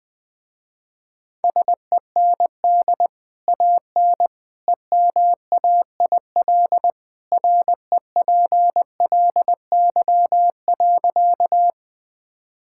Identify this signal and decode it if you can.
Morse code, heard as a keyed tone, about 20 wpm, 700 Hz. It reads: SEND AN EMAIL REPLY.
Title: Easy Sentences with Hints - 20wpm